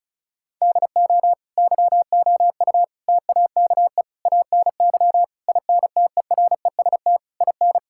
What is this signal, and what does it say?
Morse code, heard as a keyed tone, about 36 wpm, 700 Hz. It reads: DO YOU TAKE ANY INTEREST IN
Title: Sentences from Top 1000 Words - Review Part 6 - 35wpm